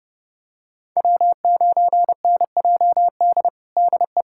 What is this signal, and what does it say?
Morse code, heard as a keyed tone, about 30 wpm, 700 Hz. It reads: W9NJD DE
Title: USA Callsign DE Callsign - 5 Characters - 30wpm